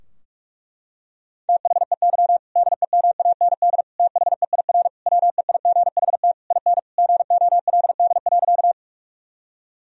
Morse code, heard as a keyed tone, about 41 wpm, 700 Hz. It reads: THEY DEMAND THEIR WEIGHT IN GOLD.
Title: Sentences from Top 1000 Words - Review Part 3 - 45wpm